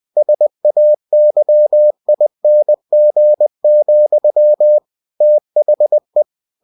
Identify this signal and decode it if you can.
Morse code, heard as a keyed tone, about 20 wpm, 600 Hz. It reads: SAYING, THE
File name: cw-books / winnie-the-pooh / winnie-20wpm-page-0024